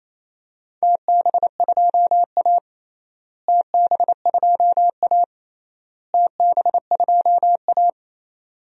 Morse code, heard as a keyed tone, about 27 wpm, 700 Hz. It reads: TB2A TB2A TB2A
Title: DXCC-100 Call Signs - 4 Characters - Repeated 3x - 28wpm